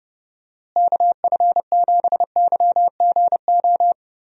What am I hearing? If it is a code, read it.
Morse code, heard as a keyed tone, about 30 wpm, 700 Hz. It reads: KF7YGO
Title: USA Call Signs - 6 Characters - Repeated 2x - 30wpm